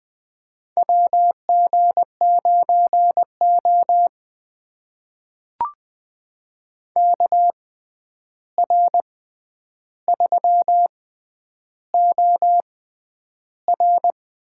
Morse code, heard as a keyed tone, about 20 wpm, spaced as for 10 wpm, 700 Hz. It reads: WG9O KR3OR
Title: USA Call Signs - 6x Character Spacing - 20wpm